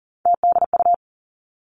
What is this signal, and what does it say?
Morse code, heard as a keyed tone, about 42 wpm, 700 Hz. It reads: TDU
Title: Three Challenging Letter-Number Combinations 40wpm